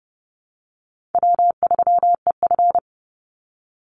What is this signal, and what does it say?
Morse code, heard as a keyed tone, about 30 wpm, 700 Hz. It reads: W3EF
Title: Call Signs Repeated 3x - 30wpm